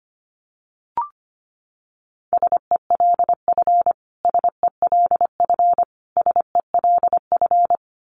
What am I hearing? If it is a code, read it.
Morse code, heard as a keyed tone, about 25 wpm, 700 Hz. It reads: SELF SELF SELF
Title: Top 200 Words Repeated 25wpm